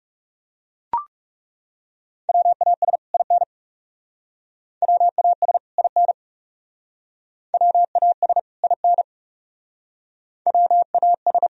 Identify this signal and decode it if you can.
Morse code, heard as a keyed tone, about 37 wpm, 700 Hz. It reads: WAS IN WAS IN WAS IN WAS
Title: Speed-Racing - Sets of 2 Words 30wpm